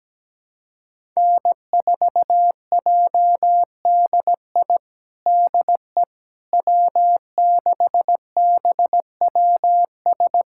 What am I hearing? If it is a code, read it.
Morse code, heard as a keyed tone, about 17 wpm, 700 Hz. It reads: N4JDI DE W6BWS
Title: USA Callsign DE Callsign - 5 Characters - 17wpm